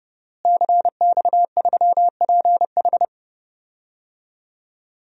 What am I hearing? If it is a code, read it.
Morse code, heard as a keyed tone, about 30 wpm, 700 Hz. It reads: CX3PH
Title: DXCC-100 Call Signs - 5 Characters - Repeated 3x - 30wpm